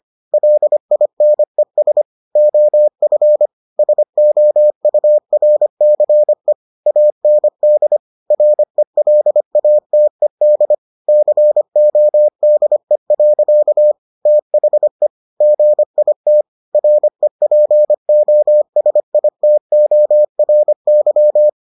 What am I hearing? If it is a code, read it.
Morse code, heard as a keyed tone, about 25 wpm, 600 Hz. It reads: LINES OF SOURCE AND RELATED CODE. THE GIT REPOSITORY